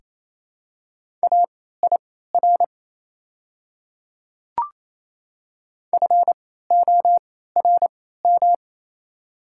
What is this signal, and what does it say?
Morse code, heard as a keyed tone, about 28 wpm, spaced as for 16 wpm, 690 Hz. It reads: AIR FORM
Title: 3x Character Spacing - Top 200 Words - 28wpm